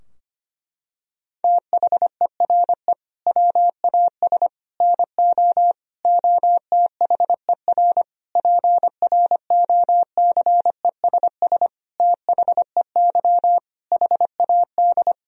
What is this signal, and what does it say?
Morse code, heard as a keyed tone, about 25 wpm, 700 Hz. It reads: THERE WAS NO OTHER PROCESS THEY HAD
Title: Sentences from Top 1000 Words - Review Part 3 - 25wpm